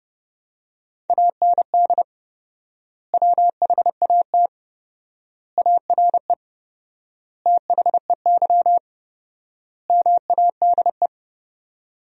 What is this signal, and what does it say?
Morse code, heard as a keyed tone, about 29 wpm, 700 Hz. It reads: AND WHAT ARE THEY MADE
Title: 4x Word-Spacing - Sentences from Top 200 Words - 30wpm